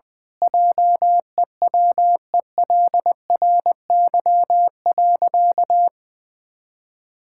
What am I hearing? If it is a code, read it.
Morse code, heard as a keyed tone, about 20 wpm, 700 Hz. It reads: JEWELRY.
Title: Pangram Challenge v2 - 20wpm